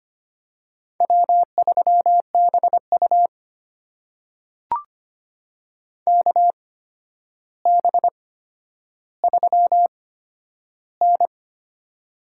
Morse code, heard as a keyed tone, about 25 wpm, spaced as for 11 wpm, 700 Hz. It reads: W3BU KB3N
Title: USA Call Signs - 4 Characters - 8x Character Spacing - 25wpm